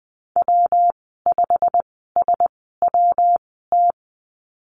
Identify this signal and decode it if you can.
Morse code, heard as a keyed tone, about 20 wpm, spaced as for 14 wpm, 700 Hz. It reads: W5SWT
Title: 122 Lesson 20g - Number 3 - Call Signs 20wpm